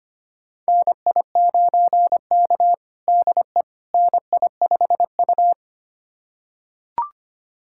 Morse code, heard as a keyed tone, about 25 wpm, 700 Hz. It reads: NI9K DE NI5U
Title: USA Callsign DE Callsign - 4 Characters - 25wpm